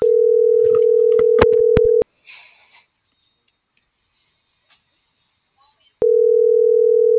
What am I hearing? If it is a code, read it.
Ringback tone (North America), 440 + 480 Hz.